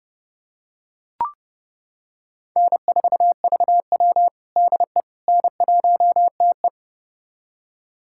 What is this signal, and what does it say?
Morse code, heard as a keyed tone, about 30 wpm, 700 Hz. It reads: N4VW DE N1TE
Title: USA Callsign DE Callsign - 4 Characters - 30wpm